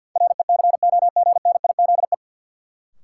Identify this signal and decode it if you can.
Morse code, heard as a keyed tone, about 53 wpm, 700 Hz. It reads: RECOGNIZE
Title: Sentences from Top 1400 Words - New Word Drill Alt - 50wpm